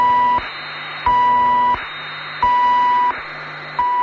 Emergency Alert System (SAME) tones, heard as harsh buzzing bursts. It is heard from 0.5 to 3.7 s.